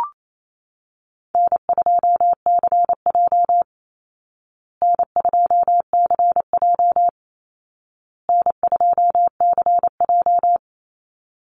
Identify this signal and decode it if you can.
Morse code, heard as a keyed tone, about 28 wpm, 700 Hz. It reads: N2CJ N2CJ N2CJ
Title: Easy Call Signs Repeated 3x - 28wpm